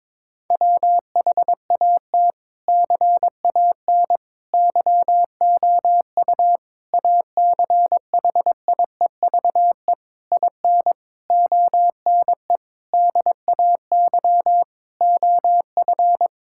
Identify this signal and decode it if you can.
Morse code, heard as a keyed tone, about 22 wpm, 700 Hz. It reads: WHAT CAN YOU ACHIEVE IN ONE DAY OF
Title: Sentences from Top 1000 Words - Review Part 6 - 22wpm